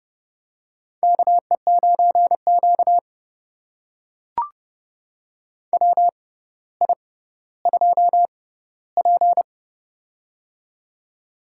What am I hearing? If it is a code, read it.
Morse code, heard as a keyed tone, about 30 wpm, spaced as for 15 wpm, 700 Hz. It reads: KE9Q WI2P
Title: USA Call Signs - 4 Characters - 6x Character Spacing - 30wpm